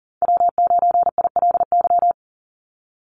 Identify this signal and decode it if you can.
Morse code, heard as a keyed tone, about 38 wpm, 700 Hz. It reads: W9ILY
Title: Call Signs - 40wpm